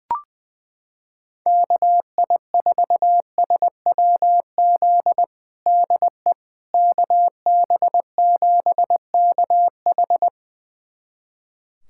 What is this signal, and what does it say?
Morse code, heard as a keyed tone, about 20 wpm, 700 Hz. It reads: KI4SWZ DE KB7KH
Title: USA Callsign DE Callsign 20wpm